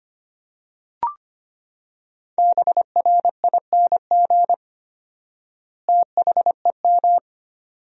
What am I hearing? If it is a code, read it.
Morse code, heard as a keyed tone, about 25 wpm, 700 Hz. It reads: BRING THEM
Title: 4x Word-Spacing - Sentences from Top 500 Words - 25wpm